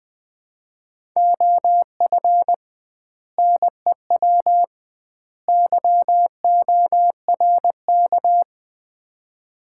Morse code, heard as a keyed tone, about 20 wpm, 700 Hz. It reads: OF NEW YORK
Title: 2x Word-Spacing - Sets of 3 Words - 20wpm